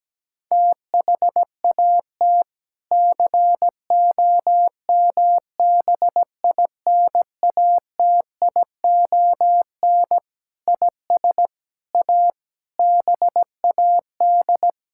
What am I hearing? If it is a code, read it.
Morse code, heard as a keyed tone, about 17 wpm, 700 Hz. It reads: THAT COMBINATION IS A BAD
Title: Sentences from Top 1000 Words - Review Part 6 - 17wpm